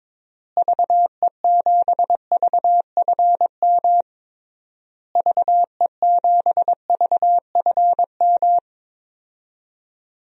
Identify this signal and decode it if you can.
Morse code, heard as a keyed tone, about 22 wpm, 700 Hz. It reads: VE7VFM VE7VFM
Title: DXCC-100 Call Signs - 6 Characters - Repeated 3x - 22wpm